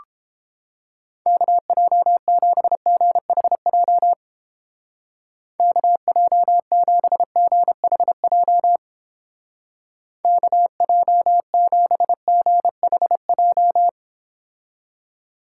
Morse code, heard as a keyed tone, about 31 wpm, 700 Hz. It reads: KJ7GHJ KJ7GHJ KJ7GHJ